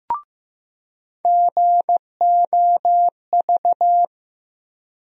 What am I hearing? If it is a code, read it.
Morse code, heard as a keyed tone, about 15 wpm, 700 Hz. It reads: GOV
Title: USA Call Sign Postfixes - 3 Letters 15wpm